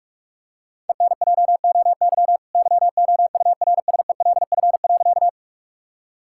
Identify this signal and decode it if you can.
Morse code, heard as a keyed tone, about 41 wpm, 700 Hz. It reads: ENJOY YOURSELF.
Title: Sentences from Top 1000 Words - Review Part 5 - 45wpm